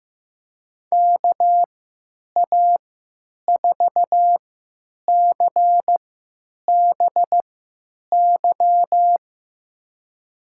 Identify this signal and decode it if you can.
Morse code, heard as a keyed tone, about 15 wpm, spaced as for 9 wpm, 700 Hz. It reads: KA4CBY
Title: USA Call Signs - 6 Characters - 3x Character Spacing - 15wpm